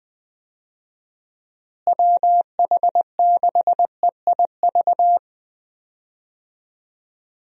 Morse code, heard as a keyed tone, about 21 wpm, 700 Hz. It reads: WH6EIV
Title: USA Call Signs - 6 Characters - Repeated 3x - 20wpm